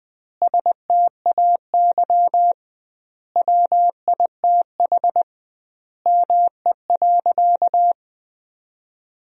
Morse code, heard as a keyed tone, about 20 wpm, 700 Hz. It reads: STAY WITH ME.
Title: 2x Word-Spacing - Sentences from Top 500 Words - 20wpm